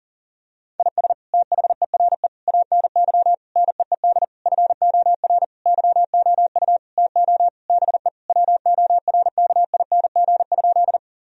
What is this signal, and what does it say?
Morse code, heard as a keyed tone, about 40 wpm, 700 Hz. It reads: IS THERE ANY NEED FOR YOU TO BE WORKING?
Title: Sentences from Top 1000 Words - Review Part 4 - 40wpm